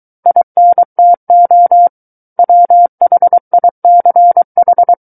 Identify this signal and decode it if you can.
Morse code, heard as a keyed tone, about 23 wpm, 700 Hz. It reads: INTO WHICH
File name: Book One / 14-Chapter 14 - 23 WPM